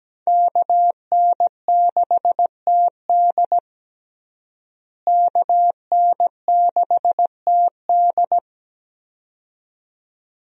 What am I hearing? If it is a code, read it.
Morse code, heard as a keyed tone, about 17 wpm, 700 Hz. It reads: KN6TD KN6TD
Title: USA Call Signs - 5 Characters - Repeated 2x - 17wpm